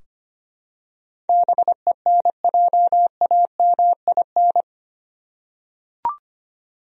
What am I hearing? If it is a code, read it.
Morse code, heard as a keyed tone, about 25 wpm, 700 Hz. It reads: BENJAMIN
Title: USA Common Names - 25wpm